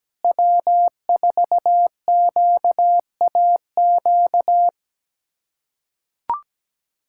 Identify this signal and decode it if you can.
Morse code, heard as a keyed tone, about 17 wpm, 700 Hz. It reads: W4QAQ
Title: USA Callsign DE Callsign - 5 Characters - 17wpm